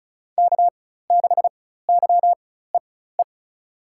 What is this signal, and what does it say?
Morse code, heard as a keyed tone, about 36 wpm, spaced as for 16 wpm, 700 Hz. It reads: K6YEE
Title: USA Call Signs - 5 Characters - 4x Character Spacing - 35wpm